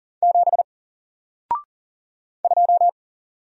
Morse code, heard as a keyed tone, about 40 wpm, 700 Hz. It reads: 7 2
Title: Single Character (Letter:Number:Punctuation:Prosign) Alt 40wpm